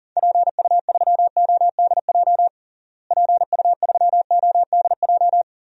Morse code, heard as a keyed tone, about 40 wpm, 700 Hz. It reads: PU3ODJ PU3ODJ
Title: DXCC-100 Call Signs - 6 Characters - Repeated 3x - 40wpm